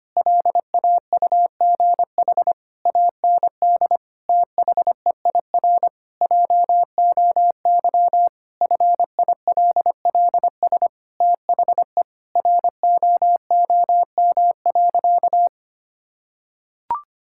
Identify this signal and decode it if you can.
Morse code, heard as a keyed tone, about 25 wpm, 700 Hz. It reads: LAUGH AND THEIR JOY FILLS THE ROOM.
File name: practice / 25wpm / Sentences with 2-Word Phrases - 25wpm